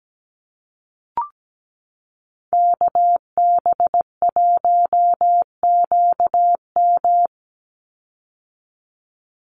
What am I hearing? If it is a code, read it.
Morse code, heard as a keyed tone, about 17 wpm, 700 Hz. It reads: KB1QM
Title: USA Call Signs 17wpm